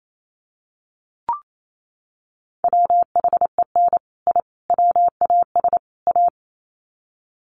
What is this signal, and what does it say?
Morse code, heard as a keyed tone, about 28 wpm, 700 Hz. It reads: WHEN I WAS A
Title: Sets of 4 Words 28wpm